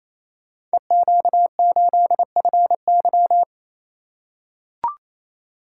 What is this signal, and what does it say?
Morse code, heard as a keyed tone, about 28 wpm, 700 Hz. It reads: EQ8FY
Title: Code Groups - 5 Letter:Numbers - 28wpm